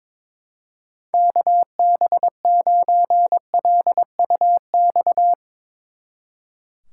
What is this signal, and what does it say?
Morse code, heard as a keyed tone, about 24 wpm, 700 Hz. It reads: KB9LUX
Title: Speed Racing - USA 2x3 Call Signs 22wpm